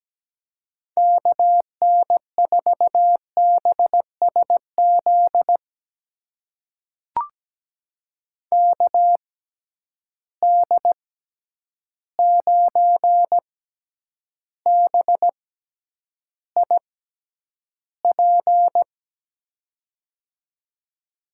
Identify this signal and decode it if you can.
Morse code, heard as a keyed tone, about 17 wpm, spaced as for 9 wpm, 700 Hz. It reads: KN4BSZ KD9BIP
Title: USA Call Signs - 6 Characters - 6x Character Spacing - 17wpm